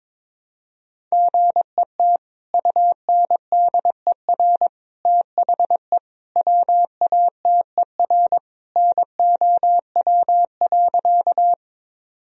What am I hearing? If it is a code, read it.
Morse code, heard as a keyed tone, about 22 wpm, 700 Hz. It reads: GET UNDER THE WATER NOW.
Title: Sentences from Top 1000 Words - Review Part 1 - 22wpm